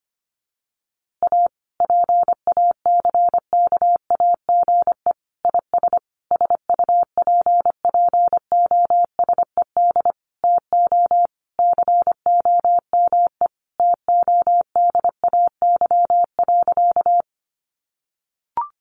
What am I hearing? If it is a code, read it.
Morse code, heard as a keyed tone, about 25 wpm, 700 Hz. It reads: A PACKAGE IS SUPPOSED TO COME TODAY.
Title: Easy Phrases 25wpm